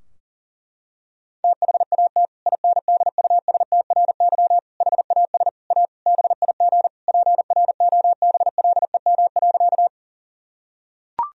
Morse code, heard as a keyed tone, about 37 wpm, 700 Hz. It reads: THAT INDUSTRY HAS A BIG PROBLEM.
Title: Sentences from Top 1000 Words - Review Part 4 - 40wpm